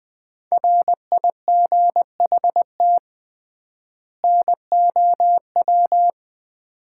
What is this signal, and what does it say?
Morse code, heard as a keyed tone, about 20 wpm, 700 Hz. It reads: RIGHT NOW
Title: 3x Word-Spacing - Sets of 2 Words - 20wpm